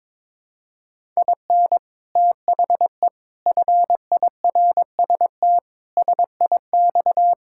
Morse code, heard as a keyed tone, about 22 wpm, 700 Hz. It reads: IN THE FIRST SIX
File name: practice / 22wpm / Sets of 6 Words - 22wpm